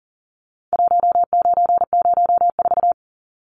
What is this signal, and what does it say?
Morse code, heard as a keyed tone, about 40 wpm, 700 Hz. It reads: 1904
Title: Numbers - 4-Digit 40wpm